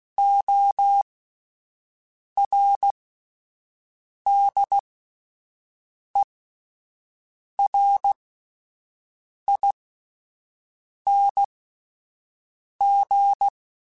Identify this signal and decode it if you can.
Morse code, heard as a keyed tone, about 16 wpm, spaced as for 5 wpm, 800 Hz. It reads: ORDERING
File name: Alphabetical / Alphabetical mp3s / Os @5